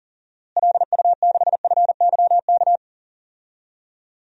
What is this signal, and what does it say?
Morse code, heard as a keyed tone, about 40 wpm, 700 Hz. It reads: LU6FYK
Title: DXCC-100 Call Signs - 6 Characters - Repeated 3x - 40wpm